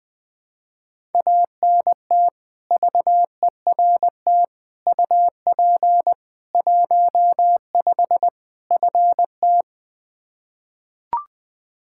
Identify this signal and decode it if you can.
Morse code, heard as a keyed tone, about 21 wpm, 700 Hz. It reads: ANT VERT UP 15 FT
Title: Practice QSO - Exchange 2 - Info - 20wpm